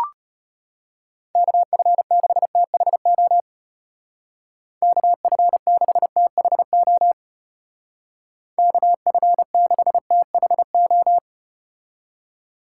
Morse code, heard as a keyed tone, about 34 wpm, 700 Hz. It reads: KF6THO KF6THO KF6THO